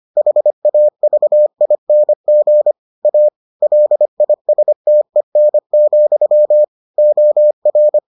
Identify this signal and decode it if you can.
Morse code, heard as a keyed tone, about 25 wpm, 600 Hz. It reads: HAVING A LISTEN, OR